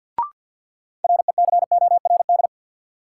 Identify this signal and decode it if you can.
Morse code, heard as a keyed tone, about 48 wpm, 700 Hz. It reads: RECORD
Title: Sentences from Top 500 Words - New Word Drill - 50wpm